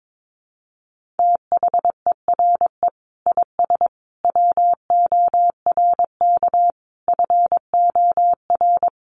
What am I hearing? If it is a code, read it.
Morse code, heard as a keyed tone, about 22 wpm, 700 Hz. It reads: THERE IS WORK FOR
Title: Sentences from Top 1400 Words 22wpm